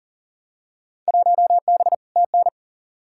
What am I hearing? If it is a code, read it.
Morse code, heard as a keyed tone, about 40 wpm, 700 Hz. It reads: 1B TN
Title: Field Day Exchanges - 40wpm